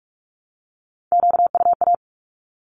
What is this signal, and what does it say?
Morse code, heard as a keyed tone, about 44 wpm, 700 Hz. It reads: QUA
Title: Q-Codes 45wpm